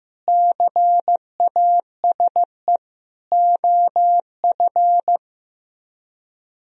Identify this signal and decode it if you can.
Morse code, heard as a keyed tone, about 15 wpm, 700 Hz. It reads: CASE OF
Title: Speed-Racing - Sets of 5 Words 15wpm